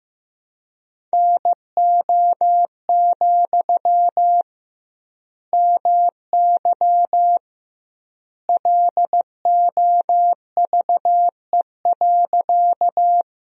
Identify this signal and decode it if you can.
Morse code, heard as a keyed tone, about 15 wpm, 700 Hz. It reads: NO, MY LOVE.
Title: 2x Word-Spacing - Sentences from Top 400 Words - 15wpm